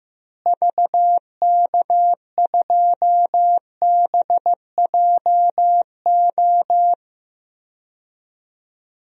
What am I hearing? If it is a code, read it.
Morse code, heard as a keyed tone, about 16 wpm, 700 Hz. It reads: VK2BJO
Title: DXCC-100 Call Signs - 6 Characters - Repeated 2x - 15wpm